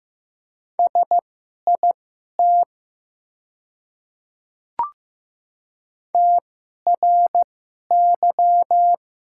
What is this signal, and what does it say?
Morse code, heard as a keyed tone, about 15 wpm, spaced as for 11 wpm, 700 Hz. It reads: SIT TRY
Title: 2x Character Spacing - Top 500 Words - 15wpm